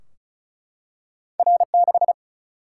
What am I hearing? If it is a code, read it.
Morse code, heard as a keyed tone, about 36 wpm, 700 Hz. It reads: R6
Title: Single-Letter Single-Number 35wpm